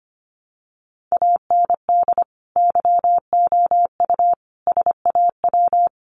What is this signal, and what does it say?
Morse code, heard as a keyed tone, about 25 wpm, 700 Hz. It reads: AND YOU SAW
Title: Sentences from Top 1300 Words 25wpm